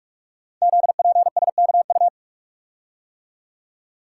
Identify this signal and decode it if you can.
Morse code, heard as a keyed tone, about 44 wpm, 700 Hz. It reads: ZPSKU
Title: Code Groups - 5 Letter:Numbers - 45wpm